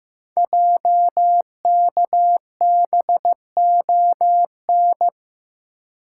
Morse code, heard as a keyed tone, about 15 wpm, 700 Hz. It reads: JKBON
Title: Code Groups - 5 Letter:Numbers - 15wpm